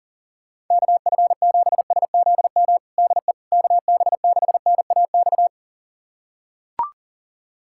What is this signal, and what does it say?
Morse code, heard as a keyed tone, about 40 wpm, 700 Hz. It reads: KF7SZM DE KB6NAX
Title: USA Callsign DE Callsign - 6 Characters - 40wpm